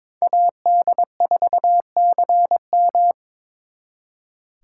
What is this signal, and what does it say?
Morse code, heard as a keyed tone, about 26 wpm, 700 Hz. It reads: AD4CM
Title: Speed Racing - USA 2x2 Call Signs 22wpm